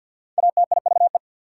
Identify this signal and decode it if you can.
Morse code, heard as a keyed tone, about 50 wpm, 690 Hz. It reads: ATIVE
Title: Prefixes and Suffixes 50wpm